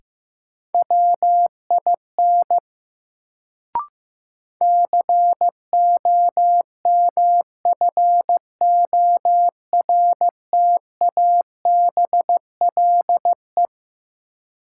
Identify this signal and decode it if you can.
Morse code, heard as a keyed tone, about 15 wpm, 700 Hz. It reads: WIN COMFORTABLE
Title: Sentences from Top 1500 Words - New Word Drill Alt - 15wpm